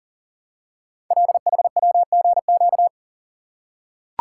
Morse code, heard as a keyed tone, about 39 wpm, 700 Hz. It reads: LHWGQ
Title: Code Groups - 5 Letters - 40wpm